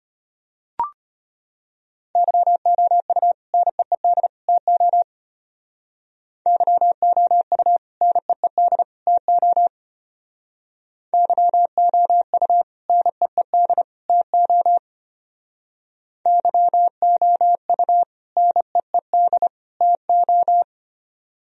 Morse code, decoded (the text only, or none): YOU NEED TO YOU NEED TO YOU NEED TO YOU NEED TO